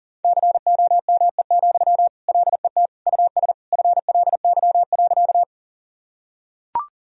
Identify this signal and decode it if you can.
Morse code, heard as a keyed tone, about 40 wpm, 700 Hz. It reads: COME, LET US FLY.
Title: Sentences from Top 1000 Words - Review Part 2 - 40wpm